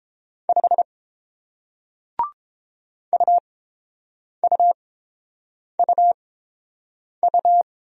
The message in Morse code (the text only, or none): 5 UUUU